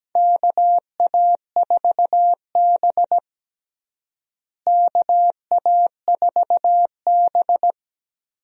Morse code, heard as a keyed tone, about 17 wpm, 700 Hz. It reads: KA4B KA4B
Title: USA Call Signs - 4 Characters - Repeated 3x - 17wpm